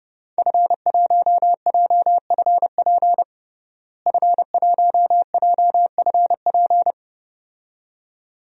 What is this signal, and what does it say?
Morse code, heard as a keyed tone, about 32 wpm, 700 Hz. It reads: F1JFP F1JFP
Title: DXCC-100 Call Signs - 5 Characters - Repeated 2x - 30wpm